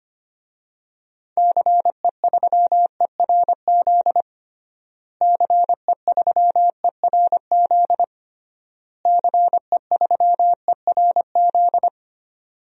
Morse code, heard as a keyed tone, about 25 wpm, 700 Hz. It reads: CE3ERZ CE3ERZ CE3ERZ
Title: DXCC-100 Call Signs - 6 Characters - Repeated 3x - 25wpm